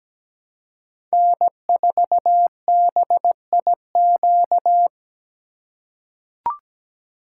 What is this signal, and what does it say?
Morse code, heard as a keyed tone, about 17 wpm, 700 Hz. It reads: N4BIQ
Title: USA Call Signs - 5 Characters - 4x Character Spacing - 17wpm